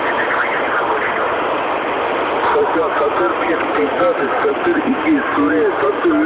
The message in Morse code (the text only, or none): none